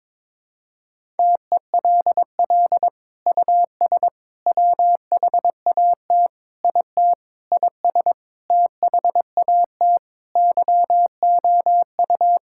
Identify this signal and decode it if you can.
Morse code, heard as a keyed tone, about 21 wpm, 700 Hz. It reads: TELL US WHAT IT IS THAT YOU
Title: Sentences from Top 1000 Words - Review Part 6 - 22wpm